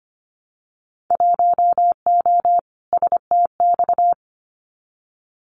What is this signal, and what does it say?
Morse code, heard as a keyed tone, about 25 wpm, 700 Hz. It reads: 1O STX
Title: Winter Field Day Exchanges 25wpm